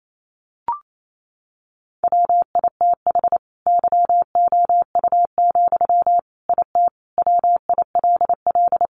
Morse code, heard as a keyed tone, about 28 wpm, 700 Hz. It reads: WITH YOU, IT WILL
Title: Sentences from Top 1400 Words 28wpm